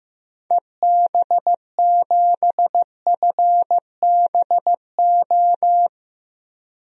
Morse code, heard as a keyed tone, about 15 wpm, 700 Hz. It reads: EB7FBO
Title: DXCC-100 Call Signs - 6 Characters - Repeated 2x - 15wpm